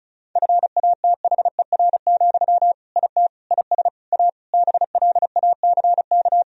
Morse code, heard as a keyed tone, about 36 wpm, 700 Hz. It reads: FATHER, IT IS A BLACK
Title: Sentences from Top 1000 Words - Review Part 2 - 35wpm